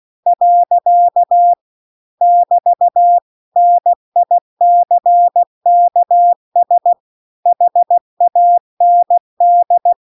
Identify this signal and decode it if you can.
Morse code, heard as a keyed tone, about 16 wpm, 700 Hz. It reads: . <BT> NICKS HAND
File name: CW-The-Big-Twoheaded-River-2-16WPM